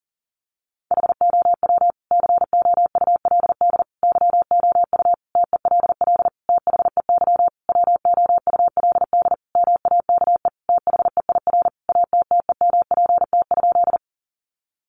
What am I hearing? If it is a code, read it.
Morse code, heard as a keyed tone, about 37 wpm, 700 Hz. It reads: HOW COULD YOU TELL THEY WOULD MAKE THEIR ATTEMPT?
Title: Sentences from Top 1000 Words 40wpm